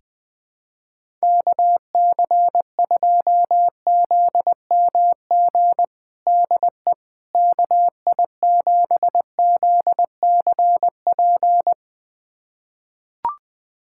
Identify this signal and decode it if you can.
Morse code, heard as a keyed tone, about 21 wpm, 700 Hz. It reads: KC2ZMG DE KI7ZCP
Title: USA Callsign DE Callsign - 6 Characters - 20wpm